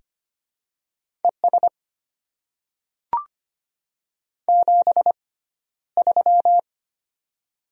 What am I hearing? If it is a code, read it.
Morse code, heard as a keyed tone, about 25 wpm, spaced as for 13 wpm, 700 Hz. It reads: ES 73
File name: Top 20 QSO Elements - 6x Character Spacing - 25wpm